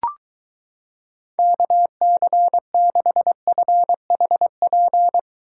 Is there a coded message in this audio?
Yes: Morse code from 1.4 to 5.2 s, heard as a keyed tone, about 23 wpm, 700 Hz. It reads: KC6FHP